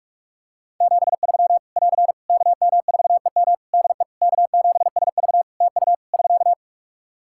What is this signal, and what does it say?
Morse code, heard as a keyed tone, about 46 wpm, 700 Hz. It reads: 73 <AR> KM4EM DE K7SV TU <SK>
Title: Practice QSO - Exchange 4 - Ending - 45wpm